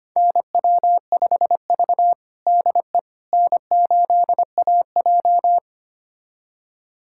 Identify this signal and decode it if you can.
Morse code, heard as a keyed tone, about 25 wpm, 700 Hz. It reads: NW5V DE N8AJ
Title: USA Callsign DE Callsign - 4 Characters - 25wpm